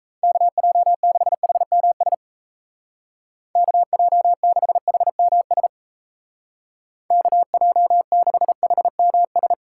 Morse code, heard as a keyed tone, about 33 wpm, 700 Hz. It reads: KJ6HMS KJ6HMS KJ6HMS
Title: Speed Racing - USA 2x3 Call Signs 28wpm